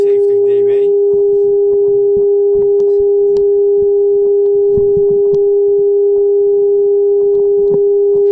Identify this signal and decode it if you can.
Telephone call progress tone (Japan), heard as a telephone network's tone, about 400 Hz. Dial tone.